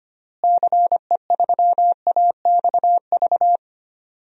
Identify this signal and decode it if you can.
Morse code, heard as a keyed tone, about 25 wpm, 700 Hz. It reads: CE3AXV
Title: DXCC-100 Call Signs - 6 Characters - Repeated 2x - 25wpm